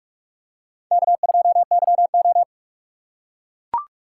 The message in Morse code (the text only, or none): K2YO